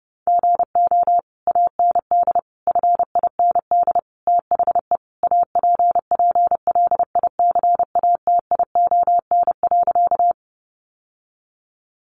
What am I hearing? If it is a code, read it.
Morse code, heard as a keyed tone, about 30 wpm, 700 Hz. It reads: GO AND FIND THE APPLICATION.
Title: Sentences from Top 1200 Words 30wpm